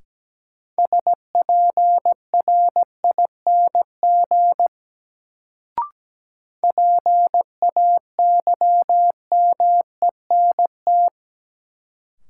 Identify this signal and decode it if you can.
Morse code, heard as a keyed tone, about 17 wpm, 700 Hz. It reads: SPRING PAYMENT
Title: Sentences from Top 1200 Words - New Word Drill Alt - 17wpm